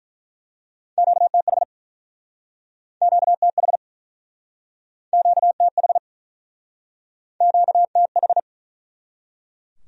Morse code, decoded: QTH QTH QTH QTH

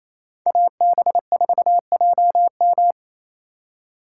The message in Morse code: AB4JM

